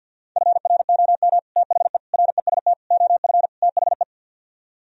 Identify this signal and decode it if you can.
Morse code, heard as a keyed tone, about 62 wpm, 700 Hz. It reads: FROM THE REST OF THE
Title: Speed-Racing - Sets of 5 Words 50wpm